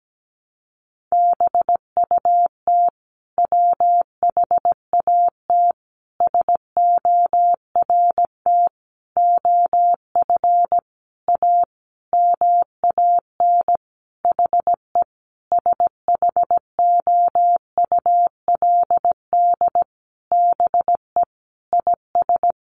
Morse code, heard as a keyed tone, about 17 wpm, 700 Hz. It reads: BUT WHAT SORT OF A MAN HE SHOULD BE IS
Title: Sentences from Top 1300 Words 17wpm